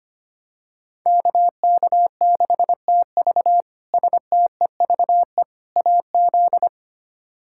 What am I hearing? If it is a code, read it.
Morse code, heard as a keyed tone, about 25 wpm, 700 Hz. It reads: KK6TV STEVE AZ
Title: K1USN SST QSO Practice - 25wpm